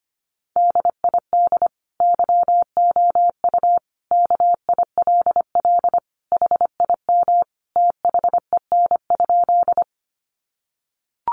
Morse code, heard as a keyed tone, about 25 wpm, 700 Hz. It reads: DID YOU KILL HIM THEN?
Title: Sentences from Top 1400 Words 25wpm